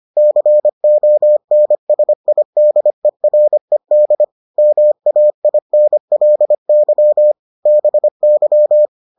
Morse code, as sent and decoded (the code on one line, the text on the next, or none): -.-. --- -. ... .. -.. .-. -.. / -- .- .. -. .-.. -.-- / -... -.--
CONSIDERED MAINLY BY